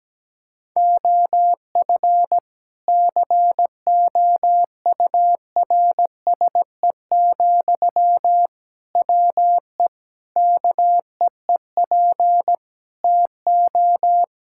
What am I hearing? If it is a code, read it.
Morse code, heard as a keyed tone, about 17 wpm, 700 Hz. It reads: OF COURSE, WE KEEP TO
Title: Sentences from Top 1600 Words 17wpm